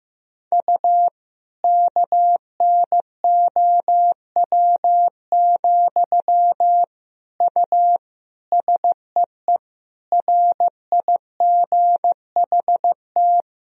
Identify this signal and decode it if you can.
Morse code, heard as a keyed tone, about 16 wpm, 700 Hz. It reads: U KNOW, U SEE RIGHT
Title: Sentences from Top 200 Words and Abbreviations 15wpm